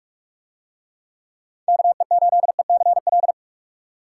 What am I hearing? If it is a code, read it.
Morse code, heard as a keyed tone, about 48 wpm, 700 Hz. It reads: KE8ECL